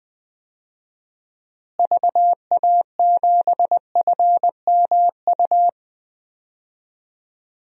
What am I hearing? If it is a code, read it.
Morse code, heard as a keyed tone, about 20 wpm, 700 Hz. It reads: VA7FMU
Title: DXCC-100 Call Signs - 6 Characters - Repeated 2x - 20wpm